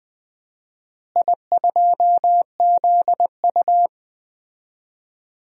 Morse code, heard as a keyed tone, about 21 wpm, 700 Hz. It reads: I2ZU
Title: DXCC-100 Call Signs - 4 Characters - Repeated 3x - 20wpm